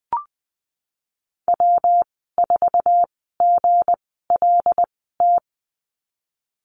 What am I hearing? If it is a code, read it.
Morse code, heard as a keyed tone, about 20 wpm, spaced as for 15 wpm, 700 Hz. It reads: W4GLT
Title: 193 Lesson 31g - Letter v - Call Signs 20wpm